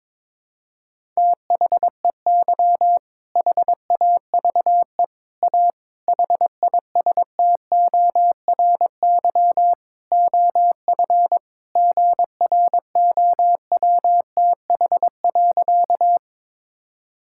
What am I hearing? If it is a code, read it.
Morse code, heard as a keyed tone, about 22 wpm, 700 Hz. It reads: THEY HAVE A HISTORY OF GROWTH.
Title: Sentences from Top 1000 Words - Review Part 5 - 22wpm